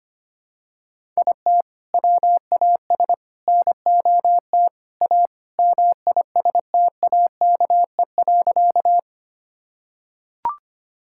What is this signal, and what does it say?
Morse code, heard as a keyed tone, about 25 wpm, 700 Hz. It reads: IT WAS NOT A MISTAKE.
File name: practice / 25wpm / Sentences with 4-Word Phrases - 25wpm